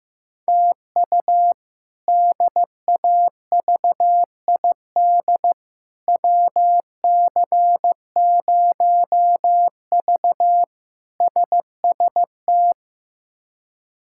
Morse code, heard as a keyed tone, about 15 wpm, 700 Hz. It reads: TU DAVID WC0V SST
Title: K1USN SST QSO Practice - 15wpm